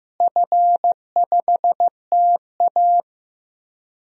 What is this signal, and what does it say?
Morse code, heard as a keyed tone, about 15 wpm, 700 Hz. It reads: F5TA
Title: DXCC-100 Call Signs - 4 Characters - Repeated 3x - 15wpm